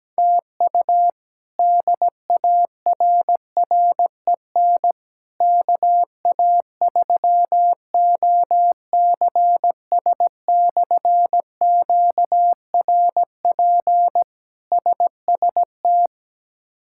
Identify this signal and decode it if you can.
Morse code, heard as a keyed tone, about 17 wpm, 700 Hz. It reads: TU DARREN KA3OCS/QRP SST
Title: K1USN SST QSO Practice - 17wpm